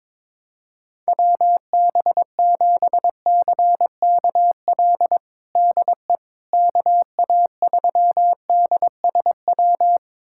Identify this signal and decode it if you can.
Morse code, heard as a keyed tone, about 22 wpm, 700 Hz. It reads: WB7CKL DE KA3DSW
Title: USA Callsign DE Callsign - 6 Characters - 22wpm